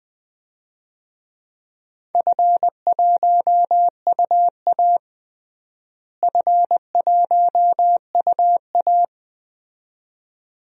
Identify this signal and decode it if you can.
Morse code, heard as a keyed tone, about 20 wpm, 700 Hz. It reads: F1UA F1UA
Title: DXCC-100 Call Signs - 4 Characters - Repeated 2x - 20wpm